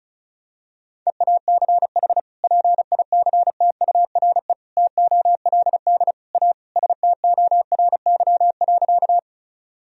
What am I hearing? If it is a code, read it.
Morse code, heard as a keyed tone, about 36 wpm, 700 Hz. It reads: EACH PICTURE TOLD A STORY.
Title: Sentences from Top 1000 Words - Review Part 1 - 35wpm